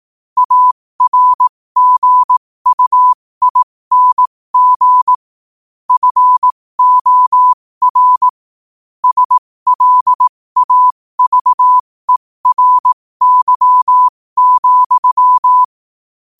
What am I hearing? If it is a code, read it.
Morse code, heard as a keyed tone, about 18 wpm, spaced as for 15 wpm, 1000 Hz. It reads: ARGUING FOR SLAVERY,
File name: qotdfiles / 15wpm